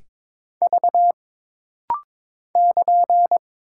Morse code, heard as a keyed tone, about 22 wpm, 700 Hz. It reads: V <KN>
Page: This rapid-fire timing is geared to push your ICR (Instant Character Recognition) skills.